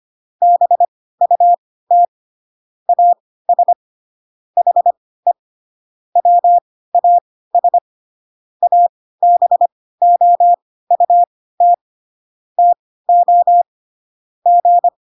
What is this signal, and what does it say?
Morse code, heard as a keyed tone, about 25 wpm, spaced as for 16 wpm, 700 Hz. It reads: BUT AS HE WAS ABOUT TO G
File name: Pinocchio / Pinocchio ch1 10 25 ik1hge